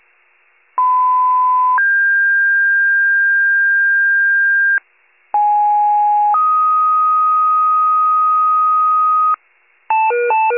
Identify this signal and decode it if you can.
Two-tone page, heard as a one-second tone, then a three-second tone, about 1010 then 1600 Hz, about 830 then 1250 Hz, with noise. It is heard from 0.8 to 9.3 s.
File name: wcalert